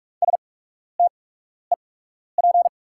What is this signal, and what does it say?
Morse code, heard as a keyed tone, about 44 wpm, spaced as for 12 wpm, 690 Hz. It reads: STEP
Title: 8x Character Spacing - Top 500 Words - 45wpm